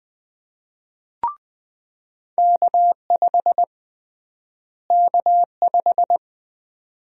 Morse code, heard as a keyed tone, about 20 wpm, 700 Hz. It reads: K5 K5
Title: USA Call Sign Prefixes - Repeated 2x - 20wpm